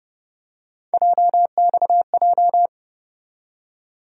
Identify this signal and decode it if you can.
Morse code, heard as a keyed tone, about 30 wpm, 700 Hz. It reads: JXJ
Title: USA Call Sign Postfixes - 3 Letters 30wpm